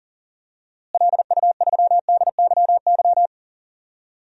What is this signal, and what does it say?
Morse code, heard as a keyed tone, about 40 wpm, 700 Hz. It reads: LU3DYY
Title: DXCC-100 Call Signs - 6 Characters - Repeated 2x - 40wpm